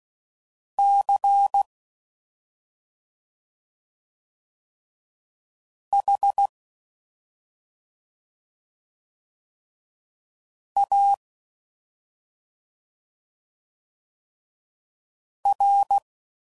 Morse code, heard as a keyed tone, about 16 wpm, 800 Hz. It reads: C H A R
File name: Suffixes / Special ING's / Sngl ING's